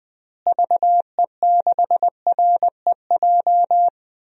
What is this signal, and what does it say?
Morse code, heard as a keyed tone, about 20 wpm, 700 Hz. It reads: VE6REJ
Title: DXCC-100 Call Signs - 6 Characters - Repeated 3x - 20wpm